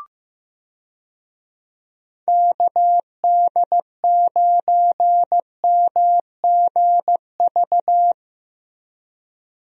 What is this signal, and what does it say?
Morse code, heard as a keyed tone, about 15 wpm, 700 Hz. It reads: KD9MGV
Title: USA Call Signs - 6 Characters - Repeated 2x - 15wpm